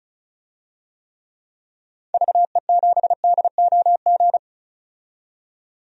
Morse code, heard as a keyed tone, about 36 wpm, 700 Hz. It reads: VE7DOG
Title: DXCC-100 Call Signs - 6 Characters - 35wpm